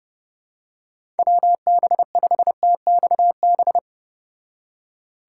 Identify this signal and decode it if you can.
Morse code, heard as a keyed tone, about 30 wpm, 700 Hz. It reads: WB5TXB